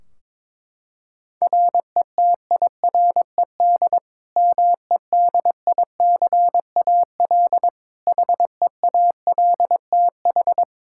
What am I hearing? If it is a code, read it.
Morse code, heard as a keyed tone, about 22 wpm, 700 Hz. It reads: RETIRED MEDICAL HEALTH
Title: Practice QSO - Exchange 3 - More Info - 22wpm